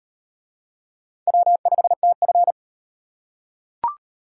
Morse code, heard as a keyed tone, about 38 wpm, 700 Hz. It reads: W5TF